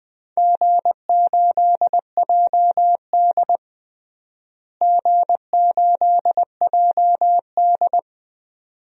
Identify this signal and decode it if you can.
Morse code, heard as a keyed tone, about 20 wpm, 700 Hz. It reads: G8JD G8JD
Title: DXCC-100 Call Signs - 4 Characters - Repeated 2x - 20wpm